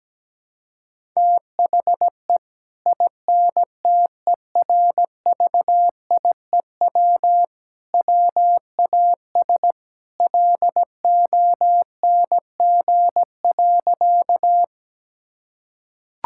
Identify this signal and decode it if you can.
Morse code, heard as a keyed tone, about 17 wpm, 700 Hz. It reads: THE INTERVIEW WAS LONG.
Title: Sentences from Top 1600 Words 17wpm